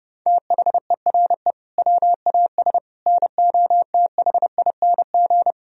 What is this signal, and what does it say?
Morse code, heard as a keyed tone, about 30 wpm, 700 Hz. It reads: THERE WAS NOTHING
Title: Sentences from Top 1000 Words - Review Part 4 - 30wpm